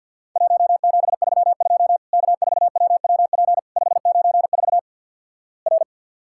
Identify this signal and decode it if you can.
Morse code, heard as a keyed tone, about 50 wpm, 700 Hz. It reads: 1732 K4WWP 594 R
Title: Worked All Europe DX Contest - QTC Exchange - 50wpm